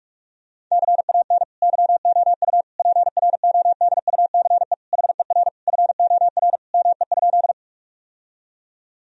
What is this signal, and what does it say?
Morse code, heard as a keyed tone, about 46 wpm, 700 Hz. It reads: CAN YOU PRODUCE HER FOR ME?
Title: Sentences from Top 1000 Words - Review Part 3 - 45wpm